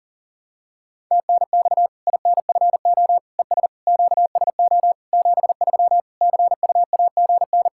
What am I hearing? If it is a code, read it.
Morse code, heard as a keyed tone, about 37 wpm, 700 Hz. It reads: TNX INFO ES QSO 73 CUAGN
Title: Practice QSO - Exchange 4 - Ending - 40wpm